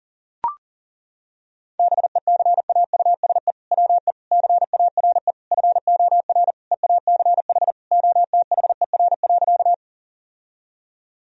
Morse code, heard as a keyed tone, about 40 wpm, 700 Hz. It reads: BECAUSE WE CARE FOR EACH OTHER.
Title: Sentences from Top 1000 Words - Review Part 3 - 40wpm